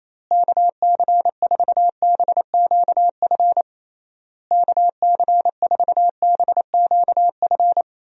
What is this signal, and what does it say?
Morse code, heard as a keyed tone, about 28 wpm, 700 Hz. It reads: KC4BQF KC4BQF
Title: USA Call Signs - 6 Characters - Repeated 3x - 28wpm